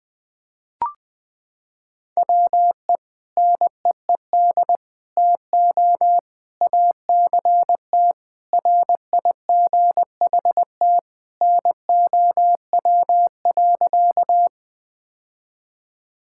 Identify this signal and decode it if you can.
Morse code, heard as a keyed tone, about 20 wpm, 700 Hz. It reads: WE NEED TO ACT RIGHT NOW.
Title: Sentences with 2-Word Phrases - 20wpm